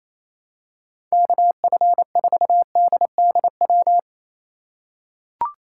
Morse code, heard as a keyed tone, about 28 wpm, 700 Hz. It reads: KF4DDW
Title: USA Call Signs - 6 Characters - 3x Character Spacing - 28wpm